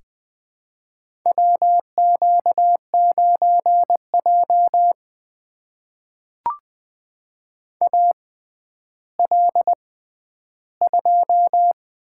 Morse code, heard as a keyed tone, about 20 wpm, spaced as for 11 wpm, 700 Hz. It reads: WQ9J AL2
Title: USA Call Signs - 4 Characters - 6x Character Spacing - 20wpm